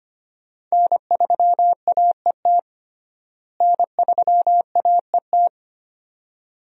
Morse code, heard as a keyed tone, about 25 wpm, 700 Hz. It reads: N3AET N3AET
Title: USA Call Signs - 5 Characters - Repeated 2x - 25wpm